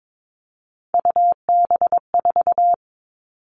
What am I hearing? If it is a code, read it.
Morse code, heard as a keyed tone, about 22 wpm, 700 Hz. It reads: UB4
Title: Three Challenging Letter-Number Combinations 22wpm